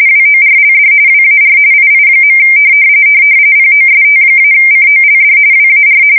RTTY, about 45.45 baud. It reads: WORLD WAR II. POST-COLD WAR
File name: fsk170 / usnavy5